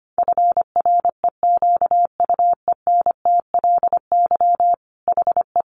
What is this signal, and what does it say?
Morse code, heard as a keyed tone, about 25 wpm, 700 Hz. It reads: FREQUENTLY HE
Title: Sentences from Top 1300 Words 25wpm